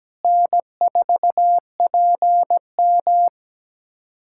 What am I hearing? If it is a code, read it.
Morse code, heard as a keyed tone, about 17 wpm, 700 Hz. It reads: N4PM
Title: USA Callsign DE Callsign - 4 Characters - 17wpm